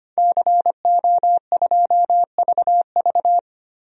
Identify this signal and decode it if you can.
Morse code, heard as a keyed tone, about 25 wpm, 700 Hz. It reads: CO2VV
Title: DXCC-100 Call Signs - 5 Characters - 25wpm